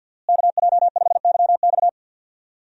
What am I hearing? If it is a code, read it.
Morse code, heard as a keyed tone, about 45 wpm, 700 Hz. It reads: KJ5YX
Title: USA Call Signs - 5 Characters - Repeated 2x - 50wpm